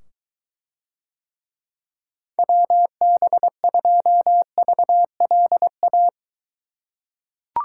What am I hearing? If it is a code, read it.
Morse code, heard as a keyed tone, about 23 wpm, 700 Hz. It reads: WB2VLA